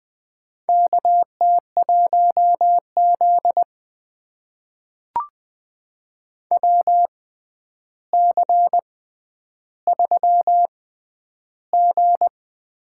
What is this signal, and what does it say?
Morse code, heard as a keyed tone, about 20 wpm, spaced as for 10 wpm, 700 Hz. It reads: KT1Z WC3G
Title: USA Call Signs - 4 Characters - 6x Character Spacing - 20wpm